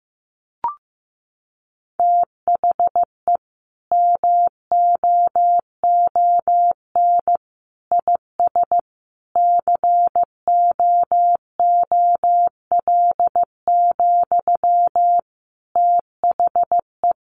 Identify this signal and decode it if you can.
Morse code, heard as a keyed tone, about 15 wpm, 700 Hz. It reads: THE MOON IS COOL, THE
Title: Sentences from Top 1200 Words 15wpm